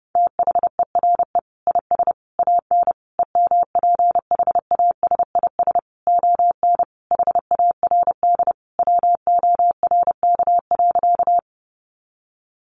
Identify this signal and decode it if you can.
Morse code, heard as a keyed tone, about 30 wpm, 700 Hz. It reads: THERE IS AN EMPHASIS ON HARD WORK.
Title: Sentences from Top 1400 Words 30wpm